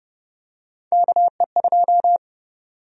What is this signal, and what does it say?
Morse code, heard as a keyed tone, about 30 wpm, 700 Hz. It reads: KE2
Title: USA Call Sign Prefixes - 30wpm